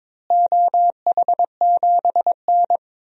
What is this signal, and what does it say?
Morse code, heard as a keyed tone, about 22 wpm, 700 Hz. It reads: OH7N
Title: DXCC-100 Call Signs - 4 Characters - Repeated 3x - 22wpm